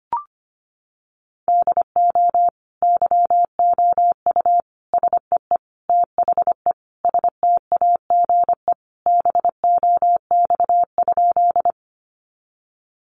Morse code, heard as a keyed tone, about 25 wpm, 700 Hz. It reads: DO YOU SEE THE STAGE BOX?
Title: Sentences from Top 1200 Words 25wpm